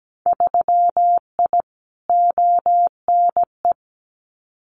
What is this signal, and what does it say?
Morse code, heard as a keyed tone, about 17 wpm, 700 Hz. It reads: 3I ONE
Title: Winter Field Day Exchanges Encore 17wpm